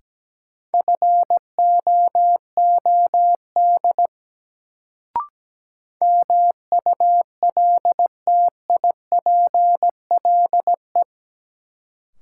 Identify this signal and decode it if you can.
Morse code, heard as a keyed tone, about 17 wpm, 700 Hz. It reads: FOOD MULTIPLE
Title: Sets of 5 Words - New Word Drill - 17wpm